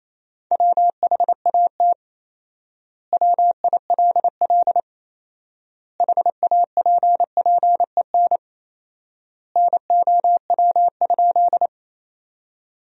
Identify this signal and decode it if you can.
Morse code, heard as a keyed tone, about 27 wpm, 700 Hz. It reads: WHAT WILL HAPPEN NOW?
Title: 4x Word-Spacing - Sentences from Top 400 Words - 28wpm